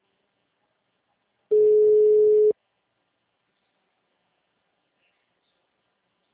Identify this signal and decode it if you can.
Ringback tone (Europe, 425 Hz).